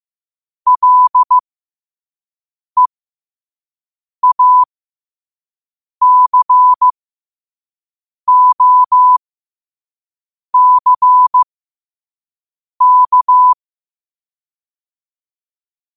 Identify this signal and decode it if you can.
Morse code, heard as a keyed tone, about 15 wpm, spaced as for 5 wpm, 1000 Hz. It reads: LEACOCK
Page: New quotes every day in morse code at 5 Words per minute.